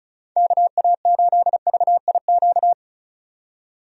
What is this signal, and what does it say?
Morse code, heard as a keyed tone, about 36 wpm, 700 Hz. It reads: KA8VIQ
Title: USA Call Signs - 6 Characters - 35wpm